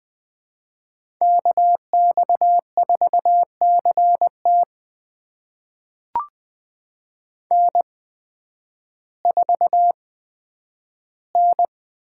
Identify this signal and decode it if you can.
Morse code, heard as a keyed tone, about 20 wpm, spaced as for 11 wpm, 700 Hz. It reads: KX4CT N4N
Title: USA Call Signs - 5 Characters - 8x Character Spacing - 20wpm